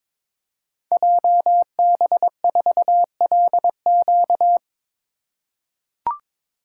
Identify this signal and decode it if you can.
Morse code, heard as a keyed tone, about 22 wpm, 700 Hz. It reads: JB4LQ
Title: Code Groups - 5 Letter:Numbers - 22wpm